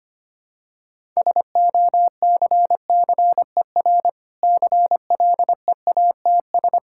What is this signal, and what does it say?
Morse code, heard as a keyed tone, about 25 wpm, 700 Hz. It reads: SOCCER CLEATS
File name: practice / 25wpm / Difficult Words with Hints - 25wpm